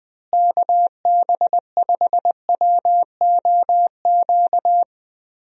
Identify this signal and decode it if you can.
Morse code, heard as a keyed tone, about 20 wpm, 700 Hz. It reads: KB5WOQ
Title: USA Call Signs - 6 Characters - Repeated 3x - 20wpm